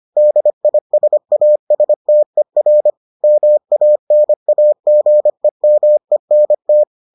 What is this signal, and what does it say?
Morse code, heard as a keyed tone, about 25 wpm, 600 Hz. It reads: DISASTER MANAGEMENT